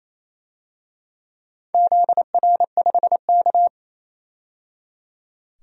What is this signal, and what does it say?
Morse code, heard as a keyed tone, about 28 wpm, 700 Hz. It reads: ZR5K
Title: DXCC-100 Call Signs - 4 Characters - 28wpm